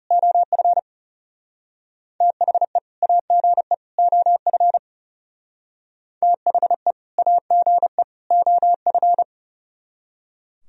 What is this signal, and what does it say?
Morse code, heard as a keyed tone, about 40 wpm, 700 Hz. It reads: OF THE AGE OF THE AGE OF